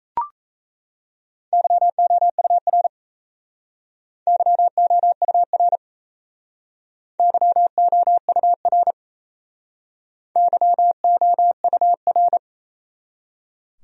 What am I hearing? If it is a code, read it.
Morse code, heard as a keyed tone, about 34 wpm, 700 Hz. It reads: YOUR YOUR YOUR YOUR
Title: Speed-Racing Top 500 Words 28wpm